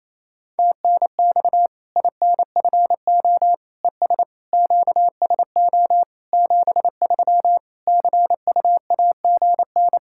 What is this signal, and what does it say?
Morse code, heard as a keyed tone, about 28 wpm, 700 Hz. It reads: TNX INFO ES QSO 73 CUAGN
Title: Practice QSO - Exchange 4 - Ending - 28wpm